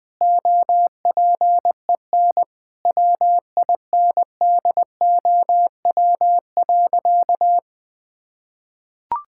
Morse code, heard as a keyed tone, about 20 wpm, 700 Hz. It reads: OPEN WINDOW.
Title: Sentences from Top 1600 Words 20wpm